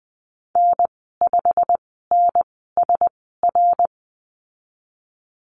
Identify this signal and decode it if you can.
Morse code, heard as a keyed tone, about 20 wpm, spaced as for 15 wpm, 700 Hz. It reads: N5NSR
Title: 167 Lesson 27g - Number 7 - Call Signs 20wpm